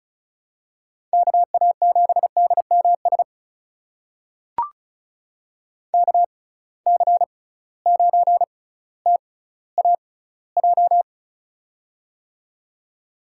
Morse code, heard as a keyed tone, about 36 wpm, spaced as for 18 wpm, 700 Hz. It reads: KA7DMS KC9TAJ
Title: USA Call Signs - 6 Characters - 6x Character Spacing - 35wpm